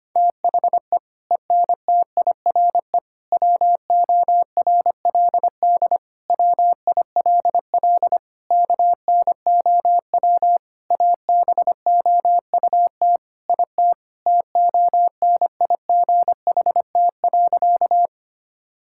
Morse code, heard as a keyed tone, about 25 wpm, 700 Hz. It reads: THE ENTIRE WORLD WILL KNOW ABOUT IT TONIGHT.
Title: Sentences from Top 1000 Words - Review Part 5 - 25wpm